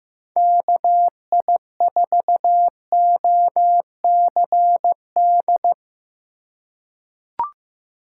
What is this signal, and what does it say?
Morse code, heard as a keyed tone, about 15 wpm, 700 Hz. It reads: KI4OCD
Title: USA Call Signs - 6 Characters - 4x Character Spacing - 15wpm